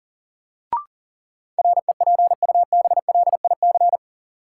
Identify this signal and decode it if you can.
Morse code, heard as a keyed tone, about 38 wpm, 700 Hz. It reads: REPUBLIC
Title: Sentences from Top 1500 Words - New Word Drill - 40wpm